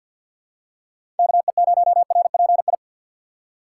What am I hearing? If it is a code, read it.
Morse code, heard as a keyed tone, about 49 wpm, 700 Hz. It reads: KE0RPI
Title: USA Call Signs - 6 Characters - 6x Character Spacing - 50wpm